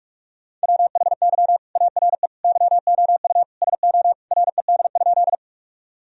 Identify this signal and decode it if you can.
Morse code, heard as a keyed tone, about 46 wpm, 700 Hz. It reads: WHY ARE YOU SO RED?
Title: Sentences from Top 1000 Words - Review Part 2 - 45wpm